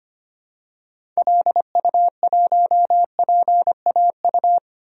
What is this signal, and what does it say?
Morse code, heard as a keyed tone, about 25 wpm, 700 Hz. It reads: LU1PAU
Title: DXCC-100 Call Signs - 6 Characters - 25wpm